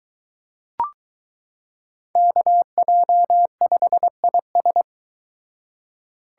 Morse code, heard as a keyed tone, about 22 wpm, 700 Hz. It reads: KJ5IS